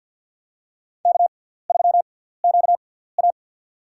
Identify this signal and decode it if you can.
Morse code, heard as a keyed tone, about 47 wpm, spaced as for 17 wpm, 690 Hz. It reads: K3QA
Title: USA Call Signs - 6x Character Spacing - 50wpm